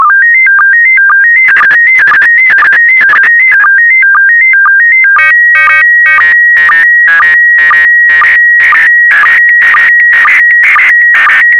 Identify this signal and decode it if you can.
SSTV (slow-scan television), PD 120.
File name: sstv / PD120-VK4MSL